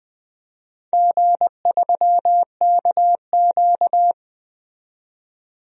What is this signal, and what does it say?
Morse code, heard as a keyed tone, about 20 wpm, 700 Hz. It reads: G3KQ
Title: DXCC-100 Call Signs - 4 Characters - 20wpm